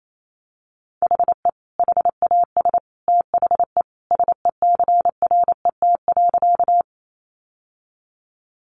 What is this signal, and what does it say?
Morse code, heard as a keyed tone, about 28 wpm, 700 Hz. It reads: HE HAS THE SECRET.
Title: Sentences from Top 1200 Words 28wpm